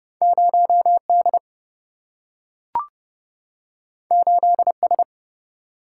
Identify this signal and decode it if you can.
Morse code, heard as a keyed tone, about 30 wpm, 700 Hz. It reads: 0D 8S
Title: Single-Letter Single-Number 30wpm